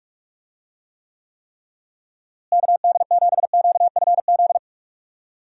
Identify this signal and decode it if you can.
Morse code, heard as a keyed tone, about 46 wpm, 700 Hz. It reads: KD7QFZ
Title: USA Call Signs - 6 Characters - Repeated 3x - 45wpm